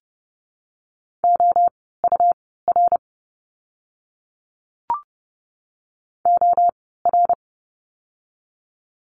Morse code, heard as a keyed tone, about 30 wpm, spaced as for 17 wpm, 700 Hz. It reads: OUR OR
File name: course / 30:10wpm / 115 Lesson 19f - Letter w - Words 30wpm